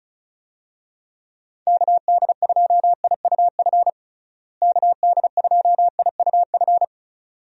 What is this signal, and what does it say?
Morse code, heard as a keyed tone, about 34 wpm, 700 Hz. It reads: KD2IUF KD2IUF
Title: USA Call Signs - 6 Characters - Repeated 2x - 35wpm